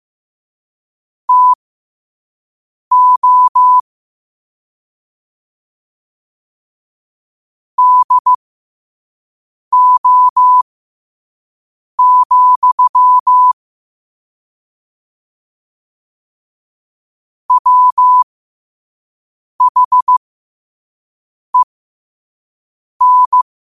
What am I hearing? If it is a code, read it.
Morse code, heard as a keyed tone, about 15 wpm, spaced as for 5 wpm, 1000 Hz. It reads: TO DO, WHEN
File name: qotdfiles / 5wpm